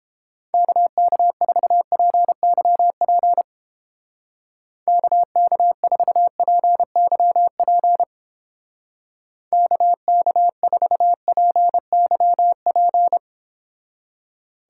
Morse code, heard as a keyed tone, about 29 wpm, 700 Hz. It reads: KK4PYP KK4PYP KK4PYP